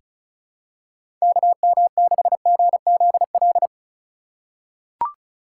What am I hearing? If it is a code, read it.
Morse code, heard as a keyed tone, about 36 wpm, 700 Hz. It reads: KM6GZL
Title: USA Call Signs - 4x Character Spacing - 35wpm